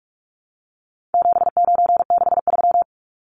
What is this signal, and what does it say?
Morse code, heard as a keyed tone, about 44 wpm, 700 Hz. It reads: 7963
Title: Numbers - 4-Digit 45wpm